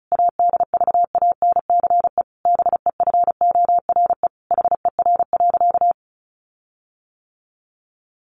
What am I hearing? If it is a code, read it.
Morse code, heard as a keyed tone, about 34 wpm, 700 Hz. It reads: ADVANCE BEFORE HER.
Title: Sentences from Top 1400 Words 35wpm